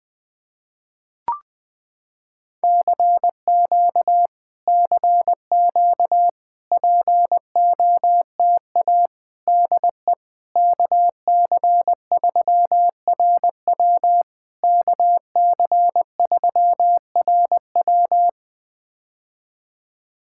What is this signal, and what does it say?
Morse code, heard as a keyed tone, about 20 wpm, 700 Hz. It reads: CQ CQ POTA DE KC3RW KC3RW
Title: POTA QSO Practice - 20wpm